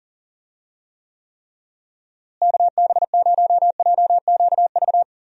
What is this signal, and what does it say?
Morse code, heard as a keyed tone, about 40 wpm, 700 Hz. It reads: KB0JQV
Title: USA Call Signs - 6 Characters - Repeated 3x - 40wpm